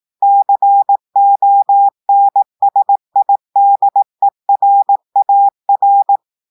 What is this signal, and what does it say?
Morse code, heard as a keyed tone, about 18 wpm, 800 Hz. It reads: CONSIDERAR
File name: portugais texte en clair 2 vitesse 18 mots minutes